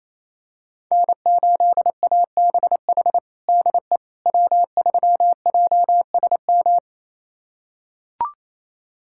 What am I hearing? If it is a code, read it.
Morse code, heard as a keyed tone, about 28 wpm, 700 Hz. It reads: N8ABH DE W3JSM
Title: USA Callsign DE Callsign - 5 Characters - 28wpm